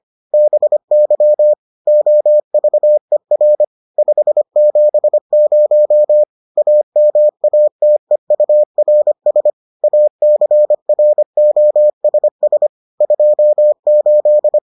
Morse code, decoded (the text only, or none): BY OVER 570 AMATEURS ACROSS 28